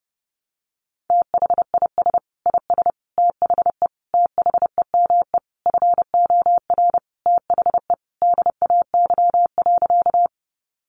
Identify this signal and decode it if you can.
Morse code, heard as a keyed tone, about 30 wpm, 700 Hz. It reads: THIS IS THE THEME FOR THE DAY.
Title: Sentences from Top 1100 Words 30wpm